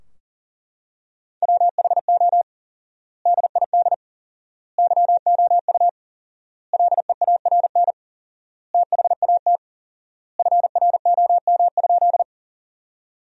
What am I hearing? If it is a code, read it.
Morse code, heard as a keyed tone, about 40 wpm, 700 Hz. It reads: WHO DID YOU LEARN THAT FROM?
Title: 4x Word-Spacing - Sentences from Top 300 Words - 40wpm